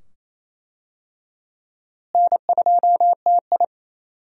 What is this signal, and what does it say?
Morse code, heard as a keyed tone, about 27 wpm, 700 Hz. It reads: N2TI
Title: USA Call Signs - 4 Characters - Repeated 2x - 28wpm